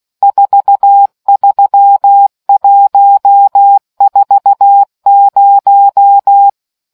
Morse code, heard as a keyed tone, about 16 wpm, 800 Hz. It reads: 43140